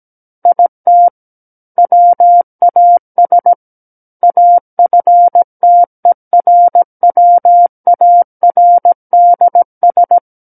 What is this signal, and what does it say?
Morse code, heard as a keyed tone, about 17 wpm, 700 Hz. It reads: IT WAS AFTERWARDS
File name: Book One / 08-Chapter 8 - 17 WPM